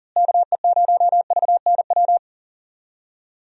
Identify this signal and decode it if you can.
Morse code, heard as a keyed tone, about 39 wpm, 700 Hz. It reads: KE0VNW
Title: USA Call Signs - 6 Characters - Repeated 3x - 40wpm